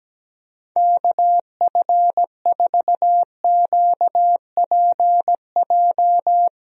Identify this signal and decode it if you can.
Morse code, heard as a keyed tone, about 17 wpm, 700 Hz. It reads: KF4QPJ
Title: USA Call Signs - 6 Characters - 17wpm